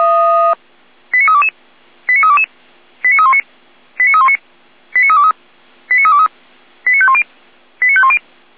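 Five-tone selective call (ZVEI), heard as quick runs of five tones, tones of about 70 ms. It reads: 89322 89322 89319 89319 89323 89323 89510 89510
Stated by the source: ZVEI five-tone selective call received on an SDR.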